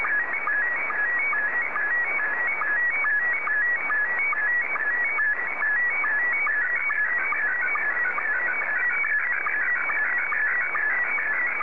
SSTV (slow-scan television), Scottie 1.